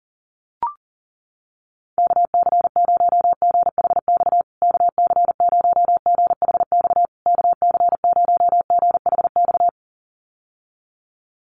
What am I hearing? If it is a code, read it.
Morse code, heard as a keyed tone, about 40 wpm, 700 Hz. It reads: KC0GHX KC0GHX KC0GHX
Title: USA Call Signs Repeated 3x 40wpm